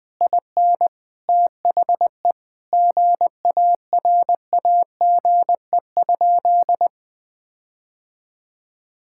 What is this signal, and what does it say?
Morse code, heard as a keyed tone, about 20 wpm, 700 Hz. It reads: IN THE GARAGE?
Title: Sentences from Top 1600 Words 20wpm